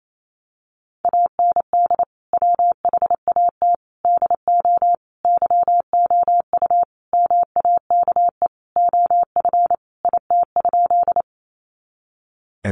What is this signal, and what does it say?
Morse code, heard as a keyed tone, about 28 wpm, 700 Hz. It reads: AND WHAT DO YOU MAKE OF IT?
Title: Sentences from Top 100 Words 28wpm